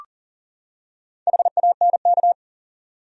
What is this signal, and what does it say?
Morse code, heard as a keyed tone, about 40 wpm, 700 Hz. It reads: HANK
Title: USA Common Names - Max 5 Letters - 40wpm